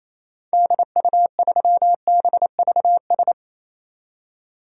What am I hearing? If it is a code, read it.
Morse code, heard as a keyed tone, about 28 wpm, 700 Hz. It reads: DU3BVS
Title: DXCC-100 Call Signs - 6 Characters - 28wpm